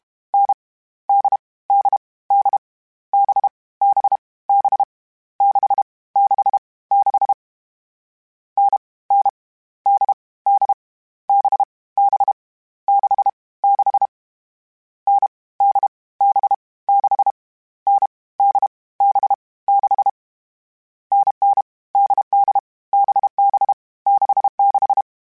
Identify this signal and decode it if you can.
Morse code, heard as a keyed tone, about 32 wpm, 800 Hz. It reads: N D D D B B B 6 6 6 N N D D B B 6 6 N D B 6 N D B 6 NN DD BB 66
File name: Variable Speed / Variable Speed-mp3s / GZ GZ7 sp @30